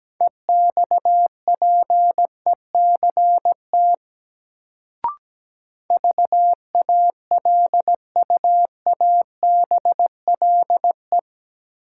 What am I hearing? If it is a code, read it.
Morse code, heard as a keyed tone, about 17 wpm, 700 Hz. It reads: EXPECT VALUABLE
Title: Sentences from Top 1100 Words - New Word Drill Alt - 17wpm